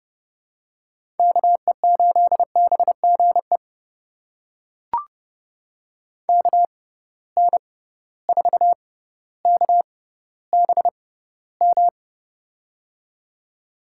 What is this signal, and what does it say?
Morse code, heard as a keyed tone, about 30 wpm, spaced as for 15 wpm, 700 Hz. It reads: KE8BGE KN4KBM
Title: USA Call Signs - 6 Characters - 6x Character Spacing - 30wpm